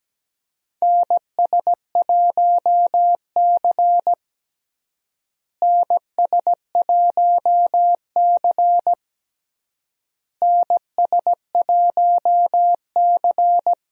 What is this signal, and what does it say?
Morse code, heard as a keyed tone, about 17 wpm, 700 Hz. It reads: NS1C NS1C NS1C
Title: USA Call Signs - 4 Characters - Repeated 3x - 17wpm